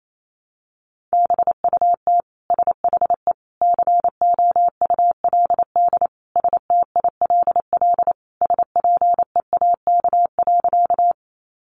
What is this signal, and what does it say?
Morse code, heard as a keyed tone, about 28 wpm, 700 Hz. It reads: BUT SHE COULD STILL SPEAK.
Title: Sentences from Top 1100 Words 28wpm